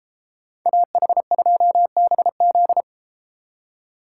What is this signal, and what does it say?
Morse code, heard as a keyed tone, about 33 wpm, 700 Hz. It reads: AH2BZ